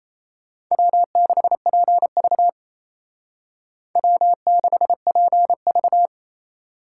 Morse code, heard as a keyed tone, about 34 wpm, 700 Hz. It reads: W6PV W6PV
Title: Speed Racing - USA 1x2 Call Signs 28wpm